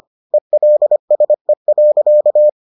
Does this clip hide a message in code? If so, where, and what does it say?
Yes: Morse code from 0.3 to 2.5 s, heard as a keyed tone, about 25 wpm, 600 Hz. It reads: ELSE.